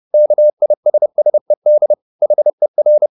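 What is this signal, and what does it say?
Morse code, heard as a keyed tone, about 30 wpm, 600 Hz. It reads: KISSED HER
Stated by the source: Morse code at 30 wpm, the novel Never by Ken Follett in English, from CW med Gnister.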